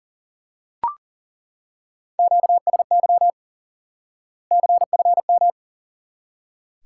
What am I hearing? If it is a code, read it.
Morse code, heard as a keyed tone, about 38 wpm, 700 Hz. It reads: QSY CFM
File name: IQE Course 3 - Lesson 89 - YAGI - Two Elements - 40wpm